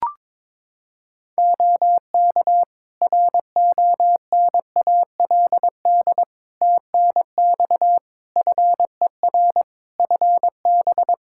Morse code, heard as a keyed tone, about 22 wpm, 700 Hz. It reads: OK RONALD TNX FER FB
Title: Practice QSO - Exchange 4 - Ending - 22wpm